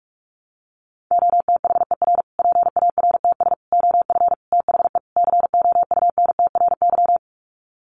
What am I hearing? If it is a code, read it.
Morse code, heard as a keyed tone, about 44 wpm, 700 Hz. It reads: OTHER PARTS OF THE COUNTRY
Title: Sets of 5 Words Encore 45wpm